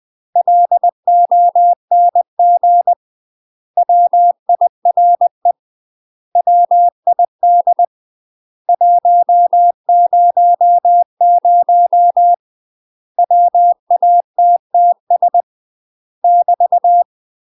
Morse code, heard as a keyed tone, about 20 wpm, 700 Hz. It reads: LONG WIRE WID 100 WATTS <BT>
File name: UK MASTER QSOs TRAINING / 20WPM / UK MASTER QSOs 0047